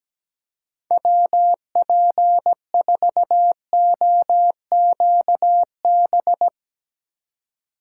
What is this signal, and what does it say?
Morse code, heard as a keyed tone, about 17 wpm, 700 Hz. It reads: WP4OQB
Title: Speed Racing - USA 2x3 Call Signs 17wpm